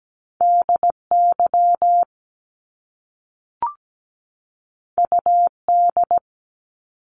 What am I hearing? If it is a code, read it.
Morse code, heard as a keyed tone, about 17 wpm, 700 Hz. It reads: DY UD
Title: Top Two Letter Combinations in Words 17wpm